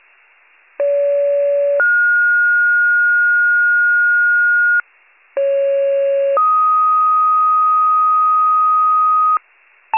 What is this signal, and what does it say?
Two-tone page, heard as a one-second tone, then a three-second tone, about 570 then 1430 Hz, about 550 then 1250 Hz, with noise. It is heard from 0.8 to 9.4 s.